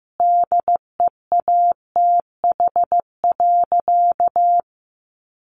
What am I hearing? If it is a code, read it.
Morse code, heard as a keyed tone, about 15 wpm, 700 Hz. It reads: DEATH.
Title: Sentences from Top 1100 Words 15wpm